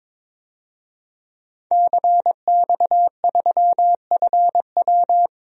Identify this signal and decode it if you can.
Morse code, heard as a keyed tone, about 22 wpm, 700 Hz. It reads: CX3FW
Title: DXCC-100 Call Signs - 5 Characters - 22wpm